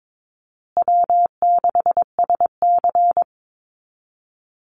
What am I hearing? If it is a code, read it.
Morse code, heard as a keyed tone, about 22 wpm, 700 Hz. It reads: W6SC
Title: Call Signs Repeated 2x - 22wpm